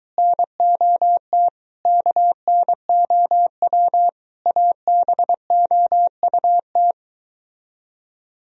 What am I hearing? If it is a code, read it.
Morse code, heard as a keyed tone, about 24 wpm, 700 Hz. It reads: NOT KNOW ABOUT